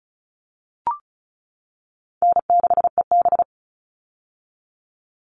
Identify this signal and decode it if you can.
Morse code, heard as a keyed tone, about 34 wpm, 700 Hz. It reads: N6EB
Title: All Cut Numbers - 4-Digit 35wpm